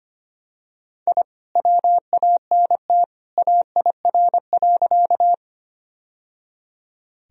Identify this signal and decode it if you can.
Morse code, heard as a keyed tone, about 25 wpm, 700 Hz. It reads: I WANT AIR.
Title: Sentences from Top 1000 Words - Review Part 1 - 25wpm